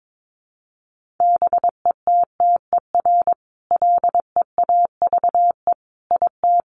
Morse code, heard as a keyed tone, about 22 wpm, 700 Hz. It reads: BETTER LEAVE IT
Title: Sentences from Top 1000 Words 22wpm